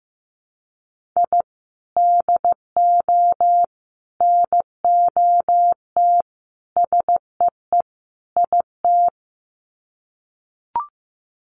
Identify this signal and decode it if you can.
Morse code, heard as a keyed tone, about 15 wpm, 700 Hz. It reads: I DO NOT SEE IT
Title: Sets of 5 Words Encore2 15wpm